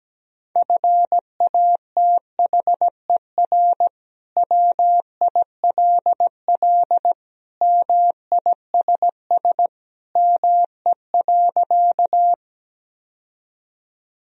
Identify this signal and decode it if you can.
Morse code, heard as a keyed tone, about 17 wpm, 700 Hz. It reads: FATHER WILL MISS ME.
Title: Sentences from Top 1000 Words - Review Part 3 - 17wpm